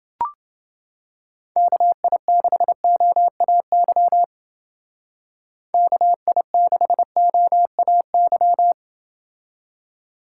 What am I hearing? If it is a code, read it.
Morse code, heard as a keyed tone, about 29 wpm, 700 Hz. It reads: KI6OAY KI6OAY